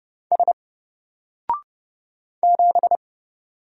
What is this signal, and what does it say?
Morse code, heard as a keyed tone, about 30 wpm, 700 Hz. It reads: S 7
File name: Single-Letter-Number-30wpm